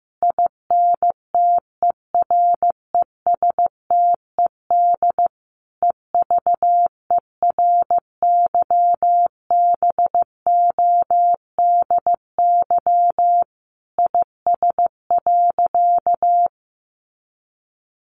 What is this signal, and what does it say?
Morse code, heard as a keyed tone, about 15 wpm, 700 Hz. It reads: INTERESTED EVERYBODY IS.
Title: Sentences from Top 1400 Words 15wpm